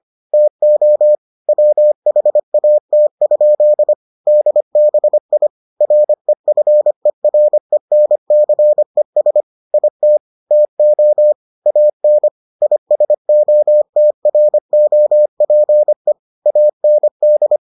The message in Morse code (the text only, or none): TO WHAT? DBI REFERENCES IT TO AN ISOTROPE AND